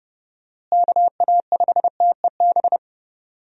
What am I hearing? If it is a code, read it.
Morse code, heard as a keyed tone, about 31 wpm, 700 Hz. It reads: KA5TEB
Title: USA Call Signs - 6 Characters - Repeated 2x - 30wpm